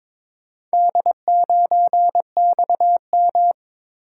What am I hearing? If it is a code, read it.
Morse code, heard as a keyed tone, about 22 wpm, 700 Hz. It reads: D9XM
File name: practice / 22wpm / Four Letter-Number Combinations - 22wpm